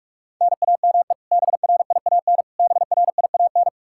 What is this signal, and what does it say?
Morse code, heard as a keyed tone, about 44 wpm, 700 Hz. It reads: NAME BRIAN BRIAN
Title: K1USN SST QSO Practice 45wpm